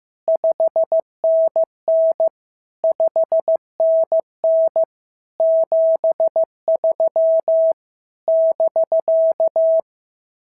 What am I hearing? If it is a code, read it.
Morse code, heard as a keyed tone, about 15 wpm, 650 Hz. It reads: 5NN 5NN 73 <BK>
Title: SOTA QSO Practice 15wpm